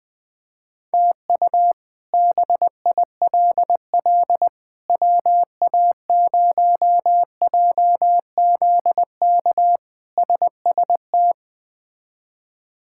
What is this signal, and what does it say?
Morse code, heard as a keyed tone, about 20 wpm, 700 Hz. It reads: TU BILL WA0JZK SST
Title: K1USN SST QSO Practice - 20wpm